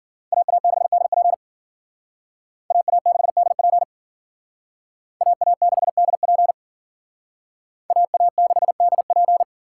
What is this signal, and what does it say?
Morse code, heard as a keyed tone, about 50 wpm, 700 Hz. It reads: AA6DP AA6DP AA6DP AA6DP